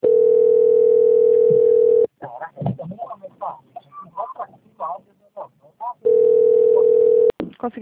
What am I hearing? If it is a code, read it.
Ringback tone (North America), 440 + 480 Hz, with speech.
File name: out- - - - - .10184